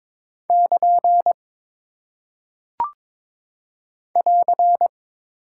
Morse code, heard as a keyed tone, about 22 wpm, 700 Hz. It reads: <KN> <AR>
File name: practice / 22wpm / Single Punctuation-Prosign - 22wpm